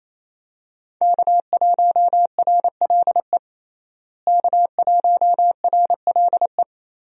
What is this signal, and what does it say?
Morse code, heard as a keyed tone, about 28 wpm, 700 Hz. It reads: K1RLE K1RLE
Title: USA Call Signs - 5 Characters - Repeated 3x - 28wpm